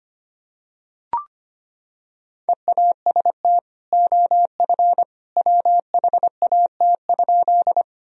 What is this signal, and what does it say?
Morse code, heard as a keyed tone, about 25 wpm, 700 Hz. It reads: EAST OF WHAT?
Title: Sentences from Top 1000 Words - Review Part 3 - 25wpm